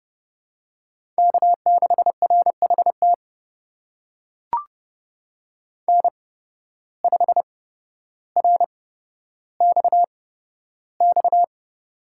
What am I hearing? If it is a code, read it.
Morse code, heard as a keyed tone, about 30 wpm, spaced as for 13 wpm, 700 Hz. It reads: K6RHT N5RXX
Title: USA Call Signs - 5 Characters - 8x Character Spacing - 30wpm